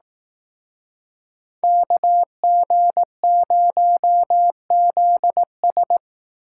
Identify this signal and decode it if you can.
Morse code, heard as a keyed tone, about 18 wpm, 700 Hz. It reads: KG0ZS
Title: Speed Racing - USA Call Signs 15wpm